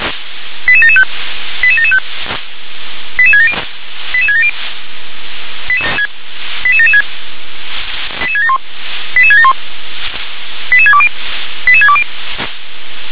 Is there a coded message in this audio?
Yes: five-tone selective call (ZVEI) from 0.7 to 12.0 s, heard as quick runs of five tones, tones of about 70 ms. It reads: 88775 88775 88670 88670 88776 88671 88671 88620 88620